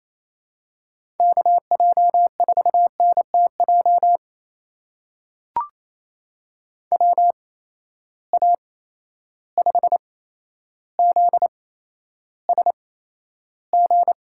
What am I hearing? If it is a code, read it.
Morse code, heard as a keyed tone, about 28 wpm, spaced as for 12 wpm, 700 Hz. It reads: KJ4NTJ WA5ZSG
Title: USA Call Signs - 6 Characters - 8x Character Spacing - 28wpm